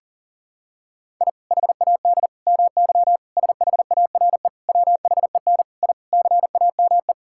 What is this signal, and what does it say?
Morse code, heard as a keyed tone, about 40 wpm, 700 Hz. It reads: I HAD MY SHARE WHEN I CAME
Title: Sentences from Top 1000 Words - Review Part 6 - 40wpm